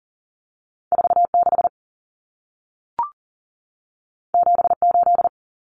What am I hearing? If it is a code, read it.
Morse code, heard as a keyed tone, about 40 wpm, 700 Hz. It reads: 46 78
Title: Numbers - 2-Digit 40wpm